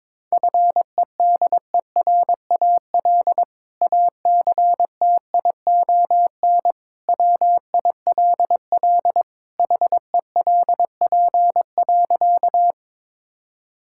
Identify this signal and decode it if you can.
Morse code, heard as a keyed tone, about 22 wpm, 700 Hz. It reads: FEDERAL ACTION WILL HELP.
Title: Sentences from Top 1000 Words - Review Part 5 - 22wpm